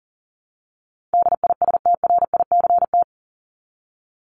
Morse code, heard as a keyed tone, about 40 wpm, 700 Hz. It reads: DISTRICT
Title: Top 1501-1750 Words 40wpm